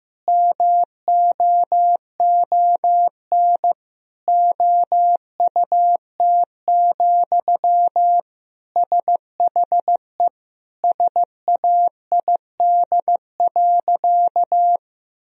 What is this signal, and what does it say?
Morse code, heard as a keyed tone, about 16 wpm, 700 Hz. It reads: MOON OUT, SHE SAID.
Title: Sentences from Top 1000 Words - Review Part 3 - 15wpm